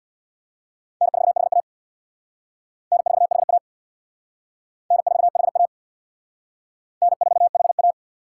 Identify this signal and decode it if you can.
Morse code, heard as a keyed tone, about 63 wpm, 700 Hz. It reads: N4HA N4HA N4HA N4HA